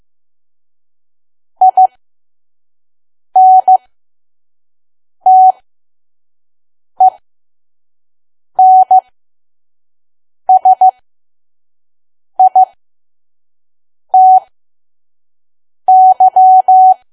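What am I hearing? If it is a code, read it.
Morse code, heard as a keyed tone, about 15 wpm, spaced as for 5 wpm, 750 Hz. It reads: INTENSITY
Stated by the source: Morse code practice at 5 WPM from KQ2P.COM.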